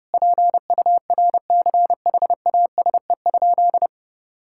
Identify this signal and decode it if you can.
Morse code, heard as a keyed tone, about 30 wpm, 700 Hz. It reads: PURCHASE?
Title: Sentences from Top 1000 Words - Review Part 5 - 30wpm